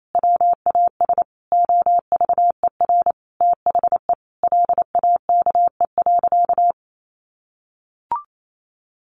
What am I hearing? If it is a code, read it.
Morse code, heard as a keyed tone, about 28 wpm, 700 Hz. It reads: WAS OVER THE LAKE.
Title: Sentences from Top 1100 Words 28wpm